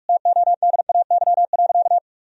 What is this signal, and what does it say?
Morse code, heard as a keyed tone, about 46 wpm, 700 Hz. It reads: TODAY.
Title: Sentences from Top 1000 Words - Review Part 4 - 45wpm